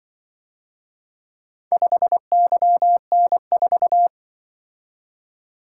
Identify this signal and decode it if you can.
Morse code, heard as a keyed tone, about 24 wpm, 700 Hz. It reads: 5YN4
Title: Speed-Racing - Four Letter-Number Combinations - 20wpm